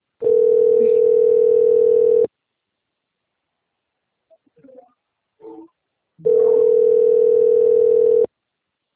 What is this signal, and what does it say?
Ringback tone (North America), 440 + 480 Hz.